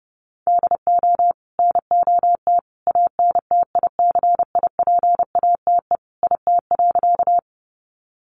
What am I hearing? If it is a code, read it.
Morse code, heard as a keyed tone, about 30 wpm, 700 Hz. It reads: DO NOT ANTICIPATE IT.
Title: Sentences from Top 1500 Words 30wpm